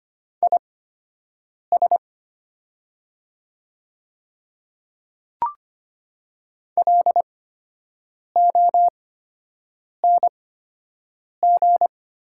Morse code, heard as a keyed tone, about 25 wpm, spaced as for 7 wpm, 700 Hz. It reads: IS LONG
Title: 8x Character Spacing - Top 100 Words - 25wpm